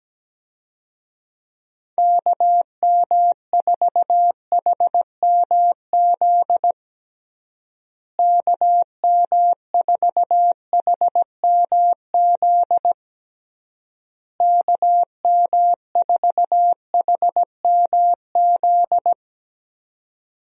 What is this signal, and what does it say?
Morse code, heard as a keyed tone, about 17 wpm, 700 Hz. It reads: KM4HMZ KM4HMZ KM4HMZ
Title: USA Call Signs - 6 Characters - Repeated 3x - 17wpm